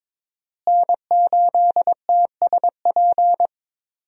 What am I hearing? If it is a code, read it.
Morse code, heard as a keyed tone, about 22 wpm, 700 Hz. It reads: N8TSP
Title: USA Call Signs - 5 Characters - Repeated 2x - 22wpm